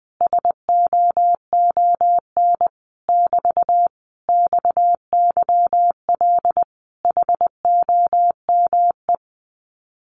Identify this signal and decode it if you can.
Morse code, heard as a keyed tone, about 20 wpm, 700 Hz. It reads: SOON <BT> XYL HOME
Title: Common QSO Phrases Encore 20wpm